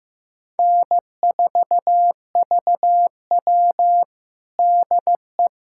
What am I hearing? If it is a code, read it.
Morse code, heard as a keyed tone, about 15 wpm, 700 Hz. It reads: N4VW DE
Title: USA Callsign DE Callsign - 4 Characters - 15wpm